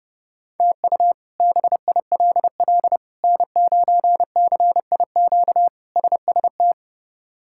Morse code, heard as a keyed tone, about 30 wpm, 700 Hz. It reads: TU BILL N9CIQ SST
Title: K1USN SST QSO Practice - 30wpm